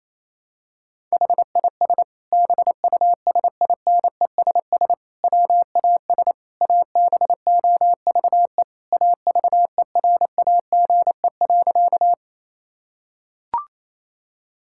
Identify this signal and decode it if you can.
Morse code, heard as a keyed tone, about 28 wpm, 700 Hz. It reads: HIS BUSINESS WAS ABOVE AVERAGE.
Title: Sentences from Top 1000 Words - Review Part 6 - 28wpm